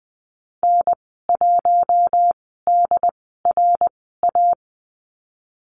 Morse code, heard as a keyed tone, about 20 wpm, spaced as for 15 wpm, 700 Hz. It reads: N1DRA
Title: 122 Lesson 20g - Number 3 - Call Signs 20wpm